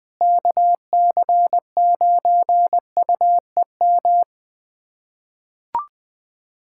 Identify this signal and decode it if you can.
Morse code, heard as a keyed tone, about 20 wpm, 700 Hz. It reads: KC9UEM
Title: USA Callsign DE Callsign - 6 Characters - 20wpm